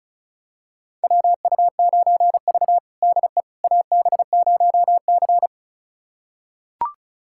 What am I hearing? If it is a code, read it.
Morse code, heard as a keyed tone, about 36 wpm, 700 Hz. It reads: WU9V DE AB0C
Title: USA Callsign DE Callsign - 4 Characters - 35wpm